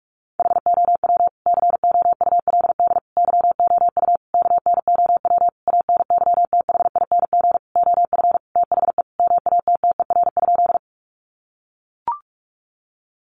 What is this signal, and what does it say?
Morse code, heard as a keyed tone, about 44 wpm, 700 Hz. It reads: HOW COULD YOU KNOW ANYTHING OF THE MATTER?
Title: Sentences from Top 1000 Words 45wpm